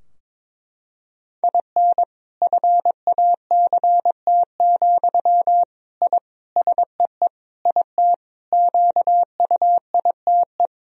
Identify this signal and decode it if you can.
Morse code, heard as a keyed tone, about 22 wpm, 700 Hz. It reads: IN FACT, I SEE IT QUITE
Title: Sentences from Top 1000 Words - Review Part 3 - 22wpm